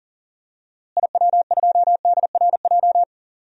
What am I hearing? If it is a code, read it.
Morse code, heard as a keyed tone, about 37 wpm, 700 Hz. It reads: IW2DRJ
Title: DXCC-100 Call Signs - 6 Characters - Repeated 3x - 40wpm